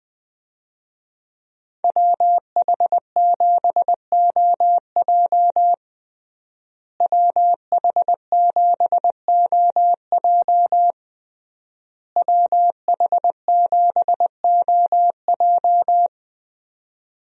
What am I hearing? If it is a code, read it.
Morse code, heard as a keyed tone, about 20 wpm, 700 Hz. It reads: WH7OJ WH7OJ WH7OJ
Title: USA Call Signs - 5 Characters - Repeated 3x - 20wpm